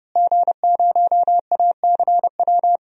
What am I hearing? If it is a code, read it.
Morse code, heard as a keyed tone, about 32 wpm, 700 Hz. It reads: G0ACW
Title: DXCC-100 Call Signs - 5 Characters - Repeated 3x - 30wpm